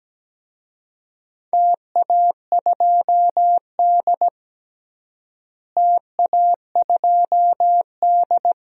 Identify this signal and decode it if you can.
Morse code, heard as a keyed tone, about 17 wpm, 700 Hz. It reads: TA2D TA2D
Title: DXCC-100 Call Signs - 4 Characters - Repeated 2x - 17wpm